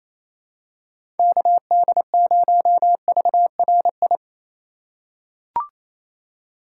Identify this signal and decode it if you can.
Morse code, heard as a keyed tone, about 27 wpm, 700 Hz. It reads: KD0VRI
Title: USA Call Signs - 6 Characters - 4x Character Spacing - 28wpm